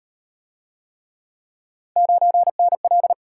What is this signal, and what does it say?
Morse code, heard as a keyed tone, about 35 wpm, 700 Hz. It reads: 9NL